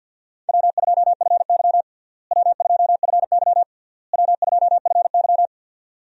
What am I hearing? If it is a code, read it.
Morse code, heard as a keyed tone, about 50 wpm, 700 Hz. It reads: W2FY W2FY W2FY
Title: USA Call Signs - 4 Characters - Repeated 3x - 50wpm